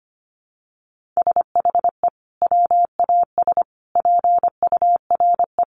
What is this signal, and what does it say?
Morse code, heard as a keyed tone, about 25 wpm, 700 Hz. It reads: SHE WAS PURE
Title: Sentences from Top 1500 Words 25wpm